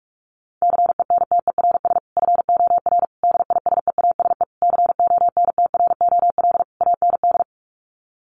Morse code, heard as a keyed tone, about 44 wpm, 700 Hz. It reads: CENTERS FOR DISEASE CONTROL AND
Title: Sets of 5 Words Encore2 45wpm